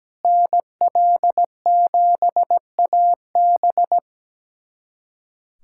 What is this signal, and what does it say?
Morse code, heard as a keyed tone, about 17 wpm, 700 Hz. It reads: NL7AB
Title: USA Callsign DE Callsign - 5 Characters - 17wpm